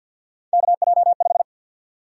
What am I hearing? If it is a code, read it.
Morse code, heard as a keyed tone, about 52 wpm, 700 Hz. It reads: KJ5
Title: USA Call Sign Prefixes - 50wpm